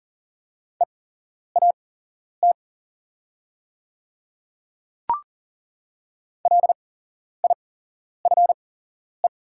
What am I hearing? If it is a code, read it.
Morse code, heard as a keyed tone, about 39 wpm, spaced as for 11 wpm, 700 Hz. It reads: EAT LIFE
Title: 8x Character Spacing - Top 300 Words - 40wpm